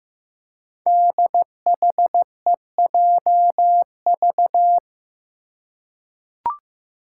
Morse code, heard as a keyed tone, about 15 wpm, 700 Hz. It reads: DHEJV
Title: Five Challenging Letter-Number Combinations 15wpm